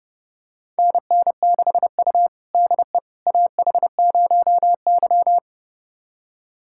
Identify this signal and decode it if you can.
Morse code, heard as a keyed tone, about 30 wpm, 700 Hz. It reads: NN6U DE AH0Y
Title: USA Callsign DE Callsign - 4 Characters - 30wpm